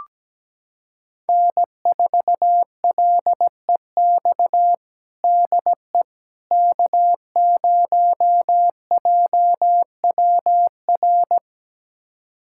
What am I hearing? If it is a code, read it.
Morse code, heard as a keyed tone, about 17 wpm, 700 Hz. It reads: N4LEX DE K0JWR
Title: USA Callsign DE Callsign - 5 Characters - 17wpm